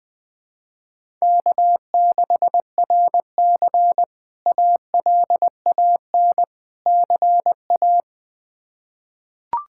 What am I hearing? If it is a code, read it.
Morse code, heard as a keyed tone, about 21 wpm, 700 Hz. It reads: K6RC ALAN CA
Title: K1USN SST QSO Practice - 20wpm